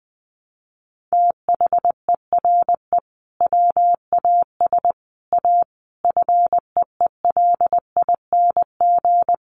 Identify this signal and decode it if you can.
Morse code, heard as a keyed tone, about 20 wpm, 700 Hz. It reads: THERE WAS A FEELING
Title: Sentences from Top 1200 Words 20wpm